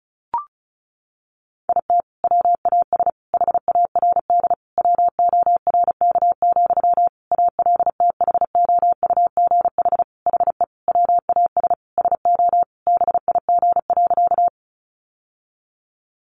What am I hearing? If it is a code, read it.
Morse code, heard as a keyed tone, about 34 wpm, 700 Hz. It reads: IT WAS HARD WORK, ALTHOUGH HE WAS SO BIG.
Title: Sentences from Top 600 Words 35wpm